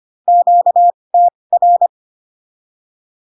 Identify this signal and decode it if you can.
Morse code, heard as a keyed tone, about 25 wpm, spaced as for 20 wpm, 700 Hz. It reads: QTR
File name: cw / Q-codes most common repeated 3 times 15 25wpm ik1hge